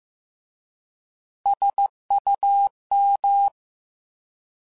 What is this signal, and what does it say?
Morse code, heard as a keyed tone, about 15 wpm, 800 Hz. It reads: SUM
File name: Alphabetical / Alphabetical mp3s / Ss @13